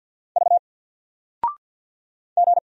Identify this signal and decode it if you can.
Morse code, heard as a keyed tone, about 53 wpm, 700 Hz. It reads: V G
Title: Single Letter-Number Alt 50wpm